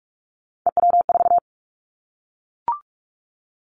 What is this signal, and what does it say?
Morse code, heard as a keyed tone, about 44 wpm, 700 Hz. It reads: EW4
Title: Callsign Prefixes for Top 100 DXCC - 45wpm